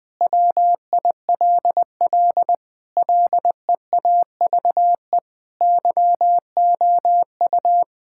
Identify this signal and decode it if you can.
Morse code, heard as a keyed tone, about 20 wpm, 700 Hz. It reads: WILL LEAVE YOU
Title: Sentences from Top 1600 Words 20wpm